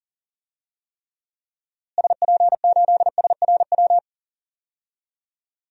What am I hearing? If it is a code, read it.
Morse code, heard as a keyed tone, about 40 wpm, 700 Hz. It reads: SP8SRW
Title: DXCC-100 Call Signs - 6 Characters - Repeated 3x - 40wpm